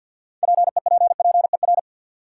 Morse code, heard as a keyed tone, about 53 wpm, 700 Hz. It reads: PEPPER
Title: Sets of 5 Words - New Word Drill - 50wpm